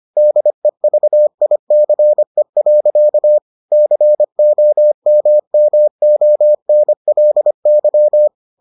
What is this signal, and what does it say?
Morse code, heard as a keyed tone, about 25 wpm, 600 Hz. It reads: DEVICE. COMMONLY